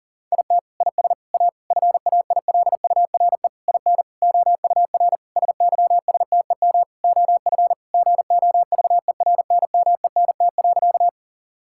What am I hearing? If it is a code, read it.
Morse code, heard as a keyed tone, about 40 wpm, 700 Hz. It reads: IT IS A FAILURE IN OUR SYSTEM OF GOVERNMENT.
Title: Sentences from Top 1000 Words - Review Part 6 - 40wpm